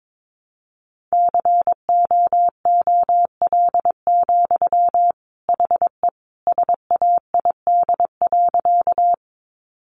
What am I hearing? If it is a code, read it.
Morse code, heard as a keyed tone, about 22 wpm, 700 Hz. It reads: COOL, HE SAID.
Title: Sentences from Top 1200 Words 22wpm